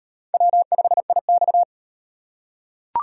Morse code, heard as a keyed tone, about 38 wpm, 700 Hz. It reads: W5IX